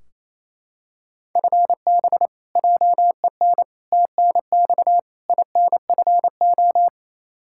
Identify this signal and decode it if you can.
Morse code, heard as a keyed tone, about 28 wpm, 700 Hz. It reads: FB JEN TNX INFO
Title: Practice QSO - Exchange 3 - More Info - 28wpm